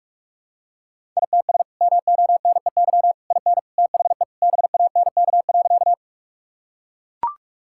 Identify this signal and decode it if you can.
Morse code, heard as a keyed tone, about 42 wpm, 700 Hz. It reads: ITS MONEY IN THE BANK.
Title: Sentences from Top 1000 Words - Review Part 6 - 45wpm